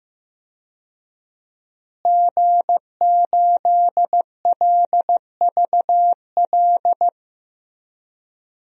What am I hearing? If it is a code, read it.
Morse code, heard as a keyed tone, about 15 wpm, 700 Hz. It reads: G8LVL
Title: DXCC-100 Call Signs - 5 Characters - Repeated 3x - 15wpm